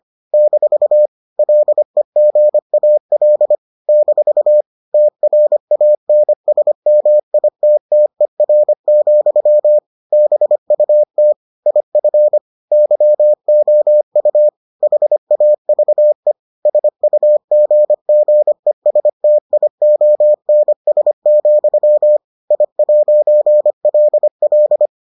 Morse code, heard as a keyed tone, about 25 wpm, 600 Hz. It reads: - LEGAL - TRANSMITTER, BUT IF YOU HAVE SUGGESTIONS, I'LL